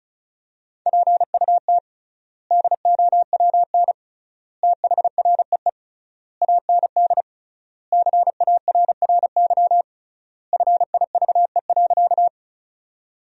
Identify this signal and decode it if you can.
Morse code, heard as a keyed tone, about 34 wpm, 700 Hz. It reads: PUT DOWN THREE AND CARRY FIVE.
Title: 3x Word-Spacing - Sentences from Top 400 Words - 35wpm